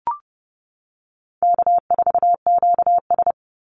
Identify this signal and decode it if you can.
Morse code, heard as a keyed tone, about 30 wpm, 700 Hz. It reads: K4QS
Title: NA SOTA Activator Callsigns 30wpm